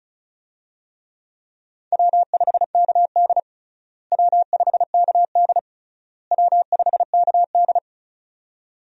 Morse code, heard as a keyed tone, about 35 wpm, 700 Hz. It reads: W5KD W5KD W5KD
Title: USA Call Signs - 4 Characters - Repeated 3x - 35wpm